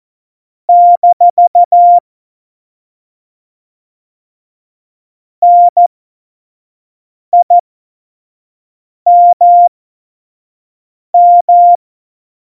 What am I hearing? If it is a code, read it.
Morse code, heard as a keyed tone, about 14 wpm, spaced as for 5 wpm, 700 Hz. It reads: - NIMM
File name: Morsekurs / Witze / Witz 21. November 2017, 25 F70